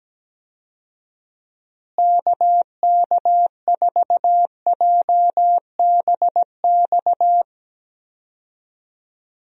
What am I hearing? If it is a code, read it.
Morse code, heard as a keyed tone, about 17 wpm, 700 Hz. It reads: KK4JBX
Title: USA Call Signs - 6 Characters - Repeated 2x - 17wpm